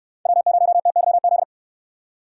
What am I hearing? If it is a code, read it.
Morse code, heard as a keyed tone, about 69 wpm, 700 Hz. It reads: W0TYZ